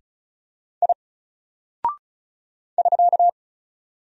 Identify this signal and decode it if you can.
Morse code, heard as a keyed tone, about 36 wpm, 700 Hz. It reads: I <SK>
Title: Single Character (Letter:Number:Punctuation:Prosign) Alt 35wpm